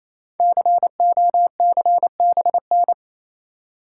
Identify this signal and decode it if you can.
Morse code, heard as a keyed tone, about 27 wpm, 700 Hz. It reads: COCBN
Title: Code Groups - 5 Letters - 28wpm